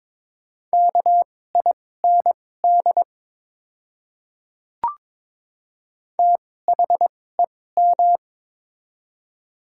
Morse code, heard as a keyed tone, about 23 wpm, spaced as for 16 wpm, 700 Hz. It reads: KIND THEM
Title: 2x Character Spacing - Top 200 Words - 22wpm